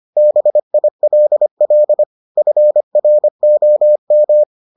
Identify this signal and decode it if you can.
Morse code, heard as a keyed tone, about 25 wpm, 600 Hz. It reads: BILL FROM